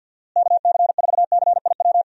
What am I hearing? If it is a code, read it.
Morse code, heard as a keyed tone, about 53 wpm, 700 Hz. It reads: KC4CIW
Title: USA Call Signs - 6 Characters - Repeated 2x - 50wpm